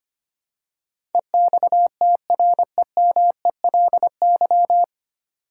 Morse code, heard as a keyed tone, about 25 wpm, 700 Hz. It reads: EXTREMELY
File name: practice / 25wpm / 9 Letter Words - First 3 Letters Given - 25wpm